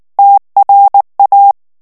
Morse code, heard as a keyed tone, about 19 wpm, 800 Hz. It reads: TRA
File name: morse mp3 / ITALIEN texte clair 2 vitesse 18 mots minute